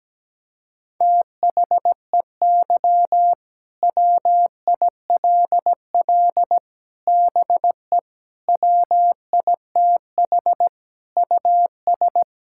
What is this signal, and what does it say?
Morse code, heard as a keyed tone, about 17 wpm, 700 Hz. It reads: THEY WILL BE WITH US
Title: Sentences from Top 1600 Words 17wpm